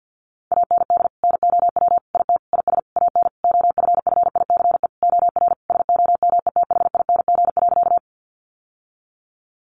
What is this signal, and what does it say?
Morse code, heard as a keyed tone, about 50 wpm, 700 Hz. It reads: AND NOW IT IS AN OFFICE OR SOMETHING.
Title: Sentences from Top 1000 Words 50wpm